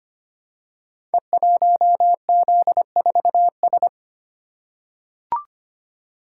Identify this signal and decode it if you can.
Morse code, heard as a keyed tone, about 25 wpm, 700 Hz. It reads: E1Z4S
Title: Code Groups - 5 Letter:Numbers - 25wpm